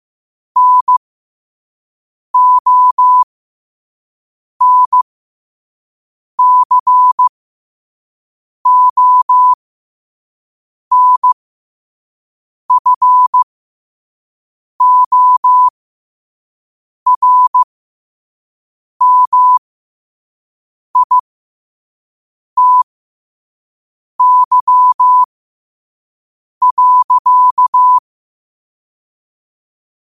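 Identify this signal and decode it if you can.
Morse code, heard as a keyed tone, about 15 wpm, spaced as for 5 wpm, 1000 Hz. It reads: NONCONFORMITY.
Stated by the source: Morse code sent at 5 wpm.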